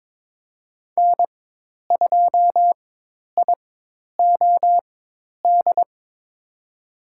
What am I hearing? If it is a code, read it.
Morse code, heard as a keyed tone, about 22 wpm, spaced as for 10 wpm, 700 Hz. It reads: N2IOD
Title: USA Call Signs - 5 Characters - 4x Character Spacing - 22wpm